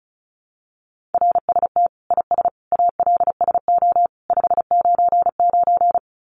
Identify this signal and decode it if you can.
Morse code, heard as a keyed tone, about 34 wpm, 700 Hz. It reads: RST IS ALSO 599
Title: Common QSO Phrases 35wpm